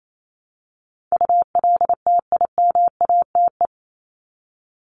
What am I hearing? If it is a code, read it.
Morse code, heard as a keyed tone, about 28 wpm, 700 Hz. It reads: ULTIMATE
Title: LY Suffix - Top 100 Words with Base Words 28wpm